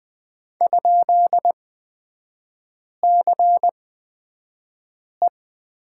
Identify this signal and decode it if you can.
Morse code, heard as a keyed tone, about 20 wpm, spaced as for 5 wpm, 700 Hz. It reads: ?CE
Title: 243 Lesson 39e - Letter z - Three Characters 20wpm